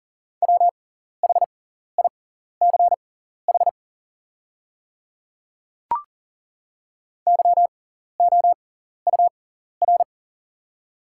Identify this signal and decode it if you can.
Morse code, heard as a keyed tone, about 40 wpm, spaced as for 14 wpm, 700 Hz. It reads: WHICH YOUR
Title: 6x Character Spacing - Top 100 Words - 40wpm